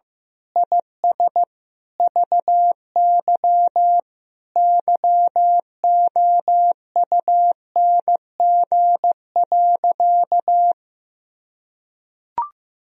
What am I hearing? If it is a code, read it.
Morse code, heard as a keyed tone, about 15 wpm, 700 Hz. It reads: IS VY YOUNG.
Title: Sentences from Top 400 Words and Abbreviations 15wpm